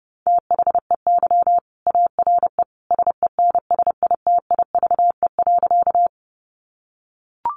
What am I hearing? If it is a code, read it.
Morse code, heard as a keyed tone, about 30 wpm, 700 Hz. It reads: THEY ARE SENSITIVE.
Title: Sentences from Top 1400 Words 30wpm